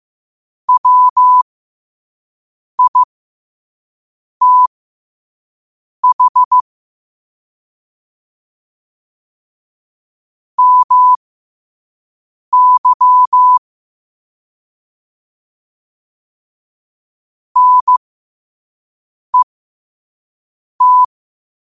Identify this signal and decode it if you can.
Morse code, heard as a keyed tone, about 15 wpm, spaced as for 5 wpm, 1000 Hz. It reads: WITH MY NET
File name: qotdfiles / 5wpm